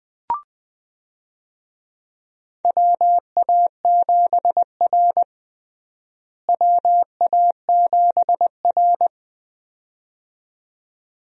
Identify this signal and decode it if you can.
Morse code, heard as a keyed tone, about 20 wpm, 700 Hz. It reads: WA7R WA7R
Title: USA Call Signs - 4 Characters - Repeated 2x - 20wpm